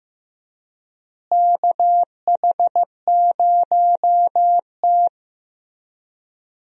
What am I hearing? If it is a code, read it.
Morse code, heard as a keyed tone, about 15 wpm, 700 Hz. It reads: KH0T
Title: USA Call Signs - 4 Characters - Repeated 3x - 15wpm